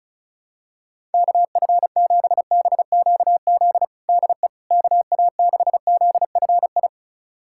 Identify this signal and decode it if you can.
Morse code, heard as a keyed tone, about 36 wpm, 700 Hz. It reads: KF7BQZ DE KA6ZFI
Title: USA Callsign DE Callsign - 6 Characters - 35wpm